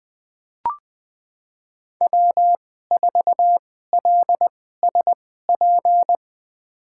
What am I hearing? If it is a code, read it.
Morse code, heard as a keyed tone, about 20 wpm, spaced as for 14 wpm, 700 Hz. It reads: W4LSP
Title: USA Call Signs - 2x Character Spacing - 20wpm